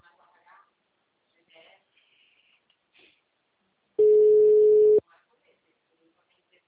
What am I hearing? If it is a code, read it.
Ringback tone (Europe, 425 Hz), with speech.